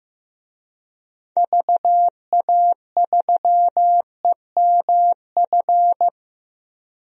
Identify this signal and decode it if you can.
Morse code, heard as a keyed tone, about 16 wpm, 700 Hz. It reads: VA3EMF
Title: DXCC-100 Call Signs - 6 Characters - 15wpm